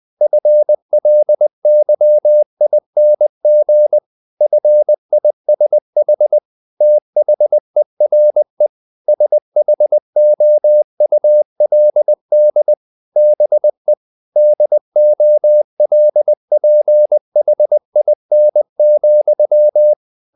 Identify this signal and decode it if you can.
Morse code, heard as a keyed tone, about 20 wpm, 600 Hz. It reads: FLYING FISH THERE SHOULD BE DOLPHIN,
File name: the old man and the sea part 2 20wpm